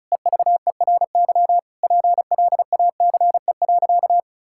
Morse code, heard as a keyed tone, about 36 wpm, 700 Hz. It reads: EVERY PLACE.
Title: Sentences from Top 1600 Words 35wpm